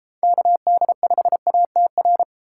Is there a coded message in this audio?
Yes: Morse code from 0.2 to 2.2 s, heard as a keyed tone, about 28 wpm, 700 Hz. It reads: KD5ATR